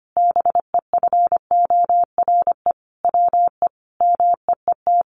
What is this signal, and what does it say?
Morse code, heard as a keyed tone, about 25 wpm, 700 Hz. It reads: BEFORE WE MEET
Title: Sentences from Top 1100 Words 25wpm